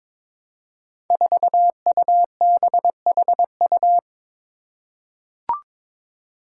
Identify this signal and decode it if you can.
Morse code, heard as a keyed tone, about 22 wpm, 700 Hz. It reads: 4UBHU
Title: Five Challenging Letter-Number Combinations 22wpm